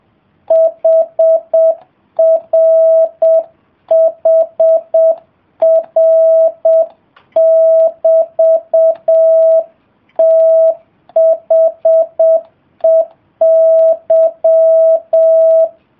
Morse code, heard as a keyed tone, about 7 wpm, 660 Hz. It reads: HRHR<BT>THEY